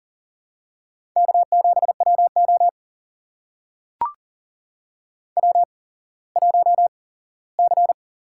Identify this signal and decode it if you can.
Morse code, heard as a keyed tone, about 40 wpm, spaced as for 19 wpm, 700 Hz. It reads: K7WO W1C
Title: USA Call Signs - 4 Characters - 8x Character Spacing - 40wpm